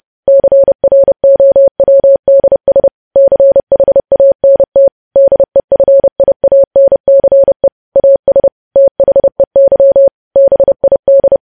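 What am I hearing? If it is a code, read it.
Morse code, heard as a keyed tone, about 30 wpm, 550 Hz. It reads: CROWDS CHANT DEFIANCE AS THEY BID
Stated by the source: Morse code news headlines sent at 30 words per minute.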